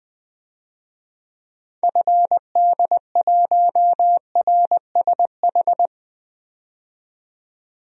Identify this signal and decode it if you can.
Morse code, heard as a keyed tone, about 20 wpm, 700 Hz. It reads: FD1RSH
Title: DXCC-100 Call Signs - 6 Characters - 20wpm